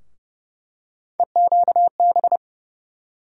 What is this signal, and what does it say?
Morse code, heard as a keyed tone, about 30 wpm, 700 Hz. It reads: EQB
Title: USA Call Sign Postfixes - 3 Letters 30wpm